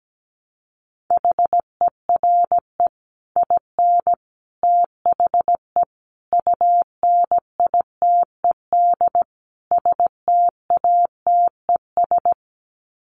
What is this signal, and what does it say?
Morse code, heard as a keyed tone, about 17 wpm, 700 Hz. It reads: HERE IN THE UNITED STATES
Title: Sets of 5 Words 17wpm